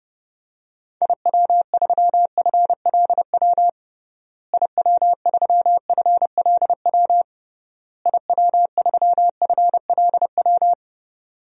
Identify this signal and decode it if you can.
Morse code, heard as a keyed tone, about 30 wpm, 700 Hz. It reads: IW3FLW IW3FLW IW3FLW
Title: DXCC-100 Call Signs - 6 Characters - Repeated 3x - 30wpm